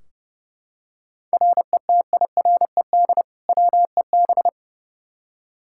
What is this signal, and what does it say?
Morse code, heard as a keyed tone, about 29 wpm, 700 Hz. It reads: RETIRED WEB
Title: Practice QSO - Exchange 3 - More Info - 30wpm